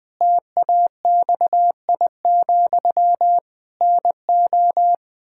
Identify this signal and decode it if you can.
Morse code, heard as a keyed tone, about 21 wpm, 700 Hz. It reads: TAXI, NO
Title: Pangram Challenge v2 - 20wpm